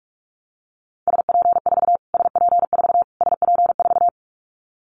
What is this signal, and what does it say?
Morse code, heard as a keyed tone, about 44 wpm, 700 Hz. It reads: SP4 SP4 SP4
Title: Callsign Prefixes for Top 25 DXCC - Repeated 3x - 45wpm